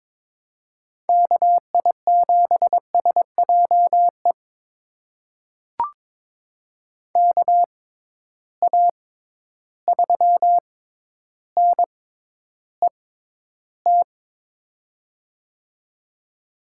Morse code, heard as a keyed tone, about 21 wpm, spaced as for 11 wpm, 700 Hz. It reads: KI7SJE KA3NET
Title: USA Call Signs - 6 Characters - 6x Character Spacing - 22wpm